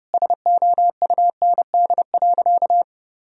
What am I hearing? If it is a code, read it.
Morse code, heard as a keyed tone, about 30 wpm, 700 Hz. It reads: SOUND.
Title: Sentences from Top 1000 Words - Review Part 1 - 30wpm